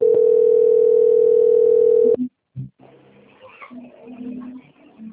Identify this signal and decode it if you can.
Ringback tone (North America), 440 + 480 Hz.